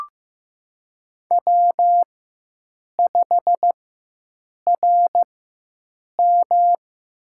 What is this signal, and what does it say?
Morse code, heard as a keyed tone, about 15 wpm, spaced as for 7 wpm, 700 Hz. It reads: W5RM
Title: USA Call Signs - 4 Characters - 4x Character Spacing - 15wpm